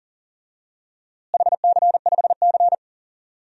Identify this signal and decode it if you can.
Morse code, heard as a keyed tone, about 40 wpm, 700 Hz. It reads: HC5C
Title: DXCC-100 Call Signs - 4 Characters - Repeated 3x - 40wpm